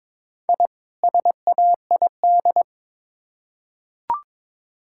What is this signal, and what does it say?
Morse code, heard as a keyed tone, about 22 wpm, 700 Hz. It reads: I SAID
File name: practice / 22wpm / Build A Phrase - 2-Word Phrases - 22wpm